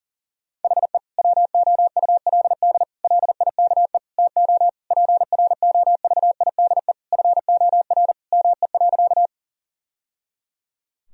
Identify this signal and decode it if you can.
Morse code, heard as a keyed tone, about 40 wpm, 700 Hz. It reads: HE WOULD LIKE TO PROVIDE FOR ME.
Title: Sentences from Top 1000 Words - Review Part 3 - 40wpm